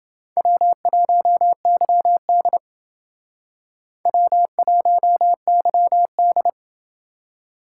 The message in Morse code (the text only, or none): W1YD W1YD